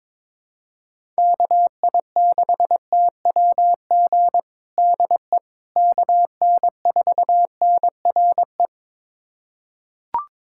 Morse code, heard as a keyed tone, about 21 wpm, 700 Hz. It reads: KI6TWG DE KN4NRE
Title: USA Callsign DE Callsign - 6 Characters - 22wpm